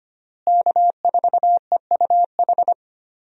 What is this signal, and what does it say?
Morse code, heard as a keyed tone, about 25 wpm, 700 Hz. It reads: K4EUH
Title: USA Call Signs - 5 Characters - Repeated 3x - 25wpm